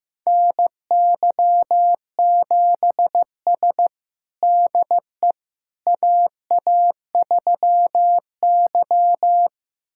Morse code, heard as a keyed tone, about 15 wpm, 700 Hz. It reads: NY7S DE AA3Y
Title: USA Callsign DE Callsign - 4 Characters - 15wpm